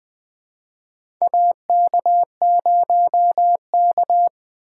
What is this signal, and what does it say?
Morse code, heard as a keyed tone, about 20 wpm, 700 Hz. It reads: AK0K
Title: USA Call Signs - 4 Characters - 3x Character Spacing - 20wpm